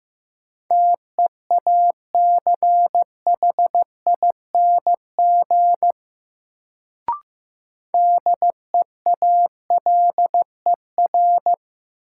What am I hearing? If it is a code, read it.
Morse code, heard as a keyed tone, about 15 wpm, 700 Hz. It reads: TEACHING DEALER
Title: Sentences from Top 1100 Words - New Word Drill - 15wpm